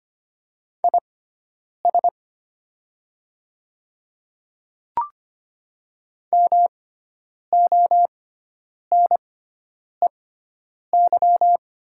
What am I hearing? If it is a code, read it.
Morse code, heard as a keyed tone, about 25 wpm, spaced as for 9 wpm, 700 Hz. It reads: IS MONEY
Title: 6x Character Spacing - Top 500 Words - 25wpm